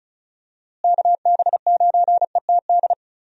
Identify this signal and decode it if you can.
Morse code, heard as a keyed tone, about 36 wpm, 700 Hz. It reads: KB9ETD
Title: USA Call Signs - 6 Characters - Repeated 3x - 35wpm